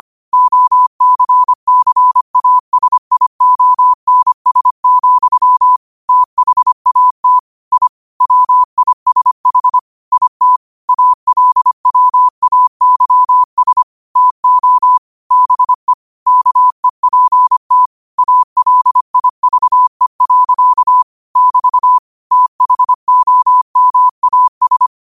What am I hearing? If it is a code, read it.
Morse code, heard as a keyed tone, about 25 wpm, 1000 Hz. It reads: OCCASIONS, THAT I WISH IT ALWAYS TO BE KEPT ALIVE. <BT> THOMAS
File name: qotdfiles / 25wpm